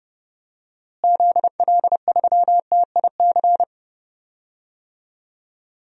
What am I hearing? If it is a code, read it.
Morse code, heard as a keyed tone, about 30 wpm, 700 Hz. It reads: ZL3TIC
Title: DXCC-100 Call Signs - 6 Characters - Repeated 2x - 30wpm